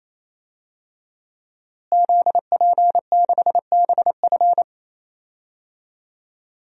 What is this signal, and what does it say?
Morse code, heard as a keyed tone, about 28 wpm, 700 Hz. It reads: ZP6BF
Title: DXCC-100 Call Signs - 5 Characters - Repeated 3x - 28wpm